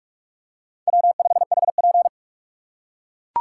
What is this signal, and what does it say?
Morse code, heard as a keyed tone, about 39 wpm, 700 Hz. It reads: W5HP